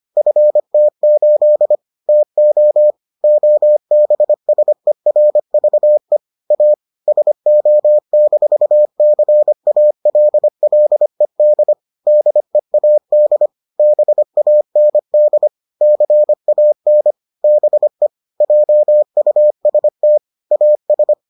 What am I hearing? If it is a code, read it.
Morse code, heard as a keyed tone, about 25 wpm, 600 Hz. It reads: FT8 TO OBSERVE A SO-CALLED DEAD BAND CAN BE JUST AS